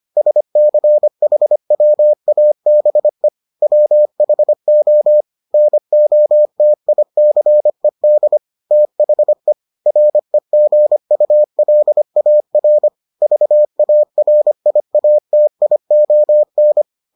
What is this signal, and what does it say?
Morse code, heard as a keyed tone, about 25 wpm, 600 Hz. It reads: SCHWABE WHO NOTICED THE REGULAR VARIATION